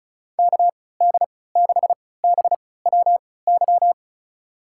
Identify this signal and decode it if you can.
Morse code, heard as a keyed tone, about 35 wpm, spaced as for 20 wpm, 690 Hz. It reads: KD6BWY
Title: USA Call Signs - 6 Characters - 3x Character Spacing - 35wpm